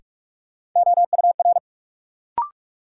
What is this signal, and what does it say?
Morse code, heard as a keyed tone, about 46 wpm, 700 Hz. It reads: OUR
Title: Sets of 2 Words - New Word Drill - 45wpm